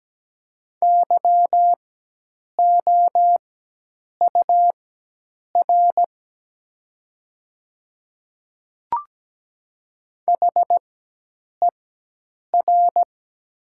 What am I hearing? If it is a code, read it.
Morse code, heard as a keyed tone, about 17 wpm, spaced as for 8 wpm, 700 Hz. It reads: YOUR HER
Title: 4x Character Spacing - Top 100 Words - 17wpm